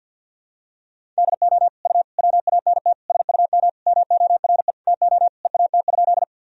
Morse code, heard as a keyed tone, about 51 wpm, 700 Hz. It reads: DO U WANT SUM MORE TO EAT?
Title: Sentences from Top 300 Words and Abbreviations 50wpm